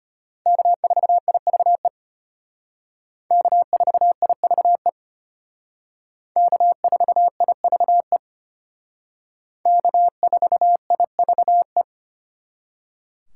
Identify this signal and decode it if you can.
Morse code, heard as a keyed tone, about 32 wpm, 700 Hz. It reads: K4IVE K4IVE K4IVE K4IVE